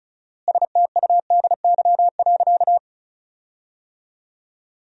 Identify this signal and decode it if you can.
Morse code, heard as a keyed tone, about 36 wpm, 700 Hz. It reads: STUDY.
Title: 4x Word-Spacing - Sentences from Top 300 Words - 35wpm